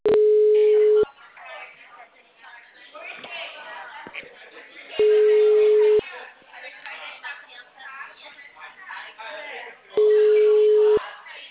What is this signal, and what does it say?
Ringback tone (Europe, 425 Hz), with speech.
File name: out- -2003- - - .11152